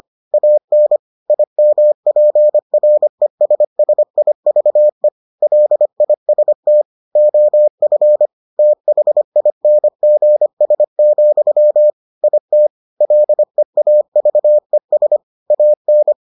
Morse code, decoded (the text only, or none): AN IMPRESSIVE LIST OF THINGS, IT LEAVES AN